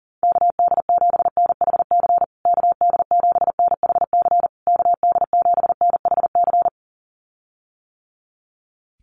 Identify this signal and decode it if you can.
Morse code, heard as a keyed tone, about 40 wpm, 700 Hz. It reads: KD7NHC KD7NHC KD7NHC
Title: USA Call Signs Repeated 3x 40wpm